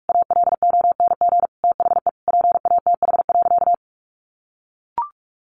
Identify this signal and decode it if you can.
Morse code, heard as a keyed tone, about 44 wpm, 700 Hz. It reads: ALONG THE PATH.
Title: A Sound Of Thunder 45wpm